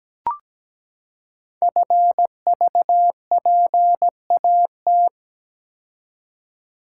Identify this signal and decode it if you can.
Morse code, heard as a keyed tone, about 17 wpm, 700 Hz. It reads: FVPAT
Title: Code Groups - 5 Letters - 17wpm